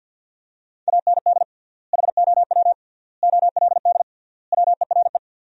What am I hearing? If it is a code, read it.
Morse code, heard as a keyed tone, about 50 wpm, 700 Hz. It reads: AND HOW OLD WERE
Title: 3x Word-Spacing - Sentences from Top 200 Words - 50wpm